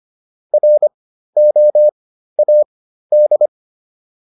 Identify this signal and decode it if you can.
Morse code, heard as a keyed tone, about 25 wpm, spaced as for 13 wpm, 600 Hz. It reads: ROAD
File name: PR402 13